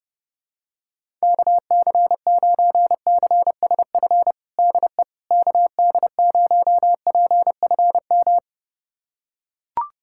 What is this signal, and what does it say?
Morse code, heard as a keyed tone, about 30 wpm, 700 Hz. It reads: KC9CSF DE KD0PFM
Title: USA Callsign DE Callsign - 6 Characters - 30wpm